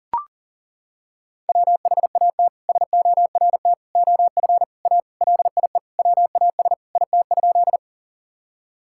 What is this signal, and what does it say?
Morse code, heard as a keyed tone, about 40 wpm, 700 Hz. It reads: WHAT SORT OF A LIE WAS IT?
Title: Sentences from Top 1600 Words 40wpm